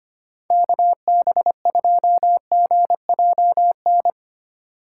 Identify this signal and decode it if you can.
Morse code, heard as a keyed tone, about 25 wpm, 700 Hz. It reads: KB2GJN
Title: USA Call Signs - 6 Characters - 8x Character Spacing - 25wpm